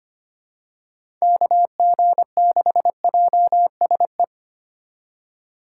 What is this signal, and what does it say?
Morse code, heard as a keyed tone, about 25 wpm, 700 Hz. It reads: KG6JSE
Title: USA Call Signs - 6 Characters - 2x Character Spacing - 25wpm